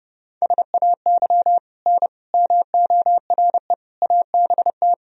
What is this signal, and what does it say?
Morse code, heard as a keyed tone, about 30 wpm, 700 Hz. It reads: SAY N MORE ABT
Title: Sentences from Top 200 Words and Abbreviations 30wpm